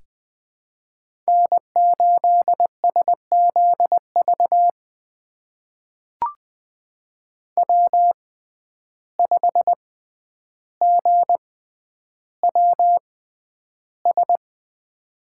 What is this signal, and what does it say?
Morse code, heard as a keyed tone, about 20 wpm, spaced as for 10 wpm, 700 Hz. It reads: N8SZV W5GWS
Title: USA Call Signs - 5 Characters - 6x Character Spacing - 20wpm